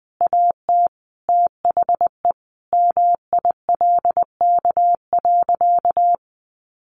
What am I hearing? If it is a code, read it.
Morse code, heard as a keyed tone, about 20 wpm, 700 Hz. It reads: AT THE MILK.
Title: Additional Easy Phrases 20wpm